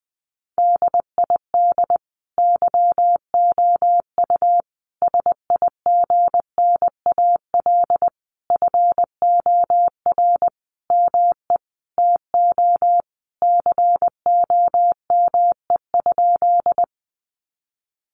Morse code, heard as a keyed tone, about 20 wpm, 700 Hz. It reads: DID YOU SIGNAL FOR ME TO COME?
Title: Sentences from Top 1400 Words 20wpm